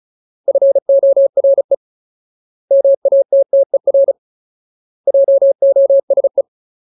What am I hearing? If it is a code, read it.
Morse code, heard as a keyed tone, about 34 wpm, 550 Hz. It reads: FORE MATTER JOSE
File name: Ultimate Word List / 35WPM Ultimate